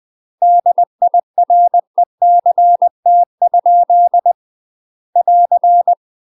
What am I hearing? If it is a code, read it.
Morse code, heard as a keyed tone, about 20 wpm, 700 Hz. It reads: DIRECT? <AR>
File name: UK MASTER QSOs TRAINING / 20WPM / UK MASTER QSOs 0052